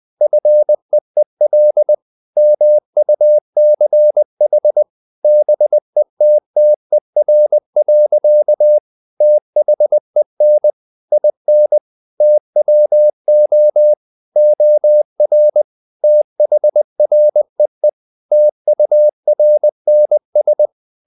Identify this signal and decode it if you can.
Morse code, heard as a keyed tone, about 20 wpm, 600 Hz. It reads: FEEL MUCH BETTER. THEN IN TWO OR THREE TURNS
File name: the old man and the sea part 2 20wpm